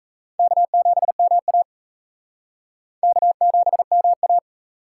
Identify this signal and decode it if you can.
Morse code, heard as a keyed tone, about 34 wpm, 700 Hz. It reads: K7MA K7MA
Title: Speed Racing - USA 1x2 Call Signs 28wpm